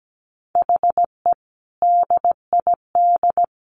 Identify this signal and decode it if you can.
Morse code, heard as a keyed tone, about 17 wpm, 700 Hz. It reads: HE DID
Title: Sentences from Top 1500 Words 17wpm